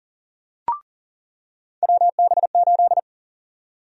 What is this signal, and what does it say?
Morse code, heard as a keyed tone, about 38 wpm, 700 Hz. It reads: WB8
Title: USA Call Sign Prefixes - 40wpm